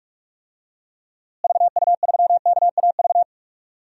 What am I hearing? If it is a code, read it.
Morse code, heard as a keyed tone, about 42 wpm, 700 Hz. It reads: VU3KAV
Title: DXCC-100 Call Signs - 6 Characters - Repeated 3x - 45wpm